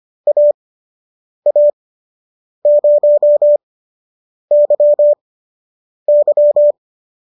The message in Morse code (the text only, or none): AA0YY